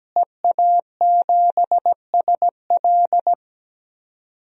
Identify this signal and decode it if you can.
Morse code, heard as a keyed tone, about 17 wpm, 700 Hz. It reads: EA7SL
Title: DXCC-100 Call Signs - 5 Characters - Repeated 3x - 17wpm